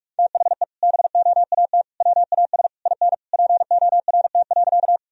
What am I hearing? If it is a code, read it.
Morse code, heard as a keyed tone, about 46 wpm, 700 Hz. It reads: THE BOAT WAS IN PORT.
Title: Sentences from Top 1000 Words - Review Part 1 - 45wpm